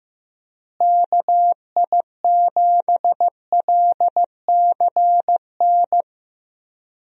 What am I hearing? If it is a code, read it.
Morse code, heard as a keyed tone, about 15 wpm, 700 Hz. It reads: KI7LCN
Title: USA Call Signs - 6 Characters - Repeated 2x - 15wpm